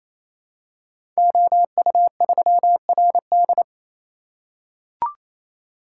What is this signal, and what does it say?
Morse code, heard as a keyed tone, about 27 wpm, 700 Hz. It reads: OU3RD
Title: Code Groups - 5 Letter:Numbers - 28wpm